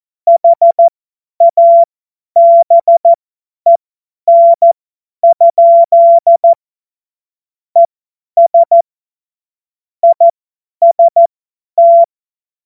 Morse code, heard as a keyed tone, about 14 wpm, spaced as for 10 wpm, 680 Hz. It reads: HABEN? ES IST
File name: Morsekurs / Witze / Witz 13 Mai 2025 50 F70